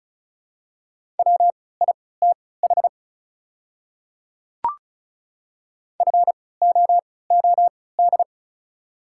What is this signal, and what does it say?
Morse code, heard as a keyed tone, about 36 wpm, spaced as for 20 wpm, 700 Hz. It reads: WITH FOOD
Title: 3x Character Spacing - Top 300 Words - 35wpm